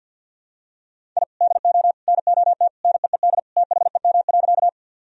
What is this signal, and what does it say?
Morse code, heard as a keyed tone, about 50 wpm, 700 Hz. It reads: I DO NOT NEED THEM.
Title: Sentences from Top 1000 Words - Review Part 1 - 50wpm